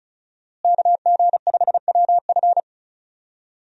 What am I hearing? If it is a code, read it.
Morse code, heard as a keyed tone, about 36 wpm, 700 Hz. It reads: KG5WF
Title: USA Call Signs - 5 Characters - Repeated 3x - 35wpm